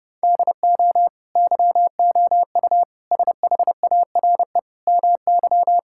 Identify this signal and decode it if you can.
Morse code, heard as a keyed tone, about 32 wpm, 700 Hz. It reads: DO YOU SHARE MY
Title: Sentences from Top 1600 Words 30wpm